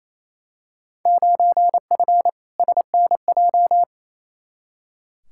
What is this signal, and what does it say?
Morse code, heard as a keyed tone, about 28 wpm, 700 Hz. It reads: 9F SNJ
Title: Field Day Exchanges Encore - 28wpm